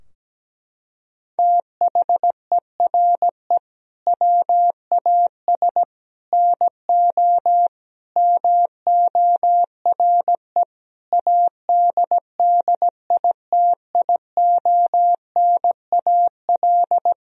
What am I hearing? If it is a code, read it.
Morse code, heard as a keyed tone, about 17 wpm, 700 Hz. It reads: THERE WAS NO MORE ADDITIONAL
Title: Sentences from Top 1000 Words - Review Part 4 - 17wpm